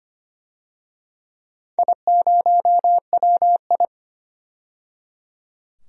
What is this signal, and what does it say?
Morse code, heard as a keyed tone, about 25 wpm, 700 Hz. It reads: I0WI
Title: DXCC-100 Call Signs - 4 Characters - 25wpm